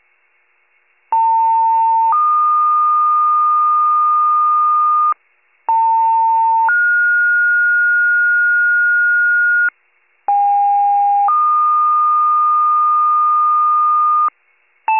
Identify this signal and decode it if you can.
Two-tone page, heard as a one-second tone, then a three-second tone, about 900 then 1220 Hz, about 900 then 1430 Hz, about 810 then 1220 Hz. It is heard from 1.1 to 14.3 s.